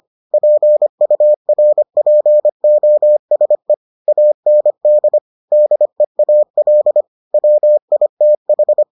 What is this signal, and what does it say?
Morse code, heard as a keyed tone, about 25 wpm, 600 Hz. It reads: PURPOSE AND DEAL WITH